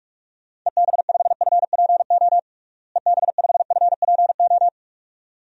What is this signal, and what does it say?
Morse code, heard as a keyed tone, about 46 wpm, 700 Hz. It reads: EB5FPO EB5FPO
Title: DXCC-100 Call Signs - 6 Characters - Repeated 2x - 45wpm